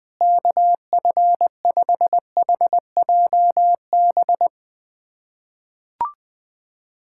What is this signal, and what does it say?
Morse code, heard as a keyed tone, about 20 wpm, 700 Hz. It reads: KF5HJB
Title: USA Callsign DE Callsign - 6 Characters - 20wpm